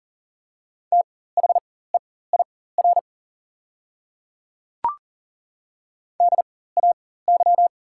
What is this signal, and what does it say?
Morse code, heard as a keyed tone, about 37 wpm, spaced as for 18 wpm, 690 Hz. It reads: THEIR DAY
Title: 4x Character Spacing - Top 100 Words - 40wpm